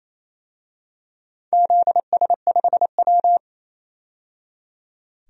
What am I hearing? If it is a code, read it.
Morse code, heard as a keyed tone, about 28 wpm, 700 Hz. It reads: ZS5W
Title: DXCC-100 Call Signs - 4 Characters - 28wpm